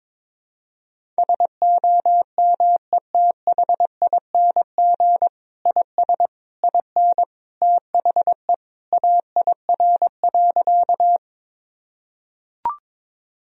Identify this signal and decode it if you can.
Morse code, heard as a keyed tone, about 22 wpm, 700 Hz. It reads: SOMETHING IS IN THE AIR.
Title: Sentences from Top 1000 Words - Review Part 1 - 22wpm